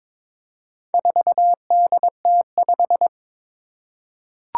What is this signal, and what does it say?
Morse code, heard as a keyed tone, about 21 wpm, 700 Hz. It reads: 4DT5
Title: Four Challenging Letter-Number Combinations 22wpm